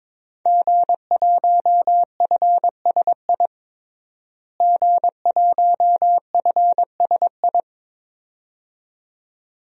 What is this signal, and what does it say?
Morse code, heard as a keyed tone, about 22 wpm, 700 Hz. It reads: G1FSI G1FSI
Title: DXCC-100 Call Signs - 5 Characters - Repeated 2x - 22wpm